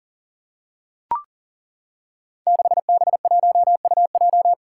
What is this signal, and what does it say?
Morse code, heard as a keyed tone, about 37 wpm, 700 Hz. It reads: 6B1UJ
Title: Five Challenging Letter-Number Combinations 40wpm